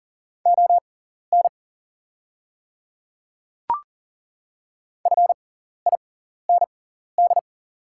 Morse code, heard as a keyed tone, about 40 wpm, spaced as for 14 wpm, 700 Hz. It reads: ON FIND
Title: 6x Character Spacing - Top 100 Words - 40wpm